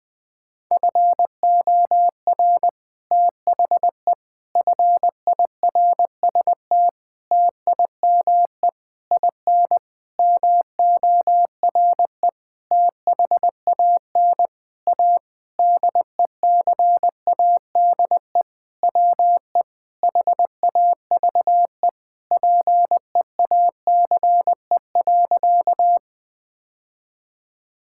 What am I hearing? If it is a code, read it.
Morse code, heard as a keyed tone, about 20 wpm, 700 Hz. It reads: FOR THE FIRST TIME IN MORE THAN A DECADE WE HAVE PEACE.
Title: Sentences with 9-Word Phrases - Spoken Beginning - 20wpm